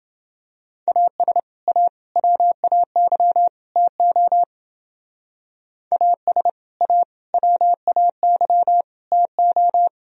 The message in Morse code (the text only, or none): AS A WAY TO AS A WAY TO